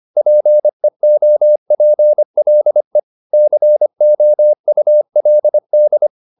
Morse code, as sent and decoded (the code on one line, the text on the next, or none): .--. --- .--. .-.. / -.-. --- ..- .-.. -..
PEOPLE COULD